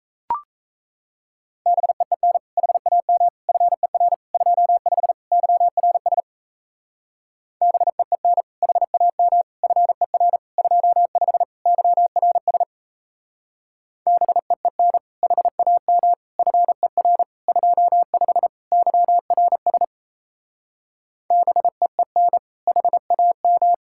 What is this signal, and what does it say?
Morse code, heard as a keyed tone, about 34 wpm, 700 Hz. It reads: BEEN HAM FER 25 YRS BEEN HAM FER 25 YRS BEEN HAM FER 25 YRS BEEN HAM
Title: Speed-Racing Common QSO Phrases Encore - 28wpm